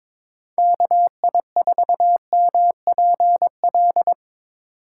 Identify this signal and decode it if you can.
Morse code, heard as a keyed tone, about 22 wpm, 700 Hz. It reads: KI4MPL
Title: USA Call Signs - 6 Characters - 4x Character Spacing - 22wpm